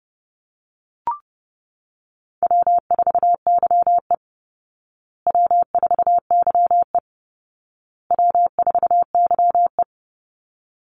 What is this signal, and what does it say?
Morse code, heard as a keyed tone, about 30 wpm, 700 Hz. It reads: W4YE W4YE W4YE
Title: Easy Call Signs Repeated 3x - 30wpm